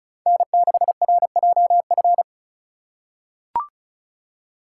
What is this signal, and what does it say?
Morse code, heard as a keyed tone, about 36 wpm, 700 Hz. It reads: N6RJF
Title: USA Call Signs - 2x Character Spacing - 35wpm